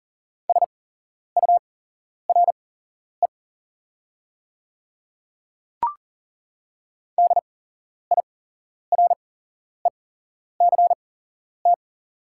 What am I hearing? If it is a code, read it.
Morse code, heard as a keyed tone, about 40 wpm, spaced as for 11 wpm, 690 Hz. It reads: SURE DIRECT
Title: 8x Character Spacing - Top 500 Words - 40wpm